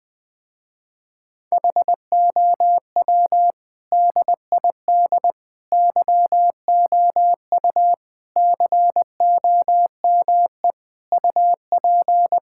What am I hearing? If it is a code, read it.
Morse code, heard as a keyed tone, about 20 wpm, 700 Hz. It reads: HOW DID YOU COME UP
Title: Sentences with 7-Word Phrases - Spoken Beginning - 20wpm